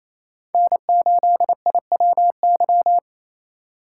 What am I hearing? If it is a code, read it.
Morse code, heard as a keyed tone, about 28 wpm, 700 Hz. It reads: N8IWY
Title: K1USN SST QSO Practice - 28wpm